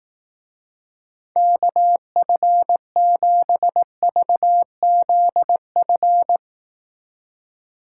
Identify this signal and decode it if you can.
Morse code, heard as a keyed tone, about 18 wpm, 700 Hz. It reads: KF7VZF
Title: Speed Racing - USA Call Signs 15wpm